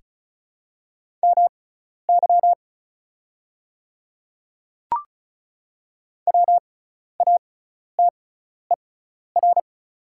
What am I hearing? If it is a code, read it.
Morse code, heard as a keyed tone, about 34 wpm, spaced as for 12 wpm, 700 Hz. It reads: MY WATER
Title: 6x Character Spacing - Top 100 Words - 35wpm